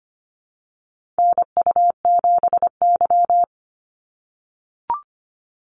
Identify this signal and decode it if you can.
Morse code, heard as a keyed tone, about 25 wpm, 700 Hz. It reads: NU7Y
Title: NA SOTA Chaser Callsigns 25wpm